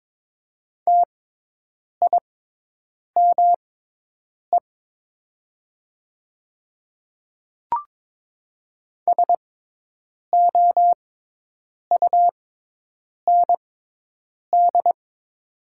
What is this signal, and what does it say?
Morse code, heard as a keyed tone, about 22 wpm, spaced as for 8 wpm, 700 Hz. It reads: TIME SOUND
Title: 6x Character Spacing - Top 100 Words - 22wpm